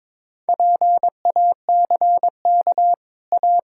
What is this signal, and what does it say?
Morse code, heard as a keyed tone, about 22 wpm, 700 Hz. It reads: PACK A
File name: practice / 22wpm / Easy Sentences with Hints - 22wpm